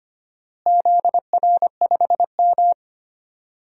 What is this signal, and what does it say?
Morse code, heard as a keyed tone, about 25 wpm, 700 Hz. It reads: ZR5M
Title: DXCC-100 Call Signs - 4 Characters - Repeated 3x - 25wpm